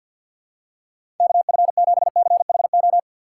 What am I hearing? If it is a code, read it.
Morse code, heard as a keyed tone, about 50 wpm, 700 Hz. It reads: KF7CHO
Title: USA Call Signs - 2x Character Spacing - 50wpm